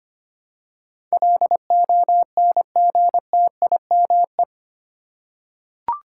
Morse code, heard as a keyed tone, about 25 wpm, 700 Hz. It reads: LONGTIME
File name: 8 Letter Words - First 3 Letters Given - 25wpm